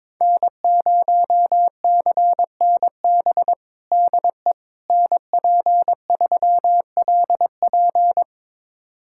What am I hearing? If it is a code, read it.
Morse code, heard as a keyed tone, about 22 wpm, 700 Hz. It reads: N0CNB DE NP3LP
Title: USA Callsign DE Callsign - 5 Characters - 22wpm